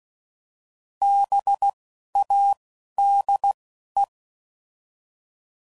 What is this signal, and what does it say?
Morse code, heard as a keyed tone, about 16 wpm, 800 Hz. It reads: B A D E